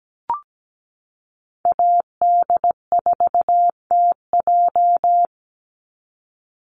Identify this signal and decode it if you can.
Morse code, heard as a keyed tone, about 17 wpm, 700 Hz. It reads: AD4TJ
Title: Call Signs - 17wpm